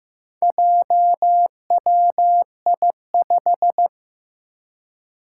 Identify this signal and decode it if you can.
Morse code, heard as a keyed tone, about 15 wpm, 700 Hz. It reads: JWI5
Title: Four Challenging Letter-Number Combinations 15wpm